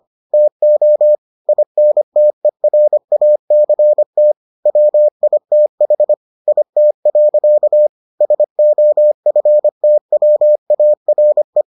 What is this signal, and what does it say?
Morse code, heard as a keyed tone, about 25 wpm, 600 Hz. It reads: TO INTERACT WITH IT. SOFTWARE